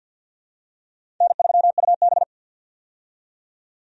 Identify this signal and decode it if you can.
Morse code, heard as a keyed tone, about 51 wpm, 700 Hz. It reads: N3UB
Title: USA Call Signs - 4 Characters - 50wpm